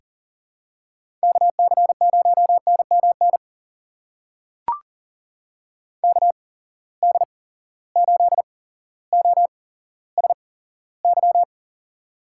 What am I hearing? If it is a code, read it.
Morse code, heard as a keyed tone, about 40 wpm, spaced as for 17 wpm, 700 Hz. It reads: KC0NMN KD8OSY
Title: USA Call Signs - 6 Characters - 8x Character Spacing - 40wpm